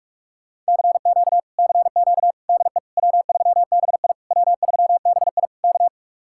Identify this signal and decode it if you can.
Morse code, heard as a keyed tone, about 59 wpm, spaced as for 52 wpm, 700 Hz. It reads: CQ CQ DE W3BI W3BI K
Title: Speed-Racing Common QSO Phrases Encore - 45wpm